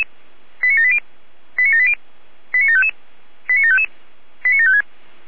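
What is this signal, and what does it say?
Five-tone selective call (ZVEI), heard as quick runs of five tones, tones of about 70 ms. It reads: 89780 89780 89755 89755 89756
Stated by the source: ZVEI five-tone selective call received by SDR, from the feuerwehr-orlamuende.org fire brigade audio server.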